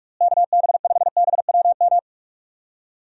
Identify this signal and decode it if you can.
Morse code, heard as a keyed tone, about 46 wpm, 700 Hz. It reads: KB5BWM
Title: USA Call Signs - 6 Characters - Repeated 2x - 45wpm